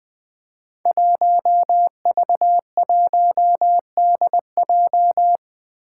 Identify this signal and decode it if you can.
Morse code, heard as a keyed tone, about 21 wpm, 700 Hz. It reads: 1V1DJ
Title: Five Challenging Letter-Number Combinations 20wpm